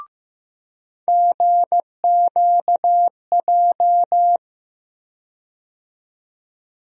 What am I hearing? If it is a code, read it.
Morse code, heard as a keyed tone, about 15 wpm, 700 Hz. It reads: GQJ
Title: USA Call Sign Postfixes - 3 Letters 15wpm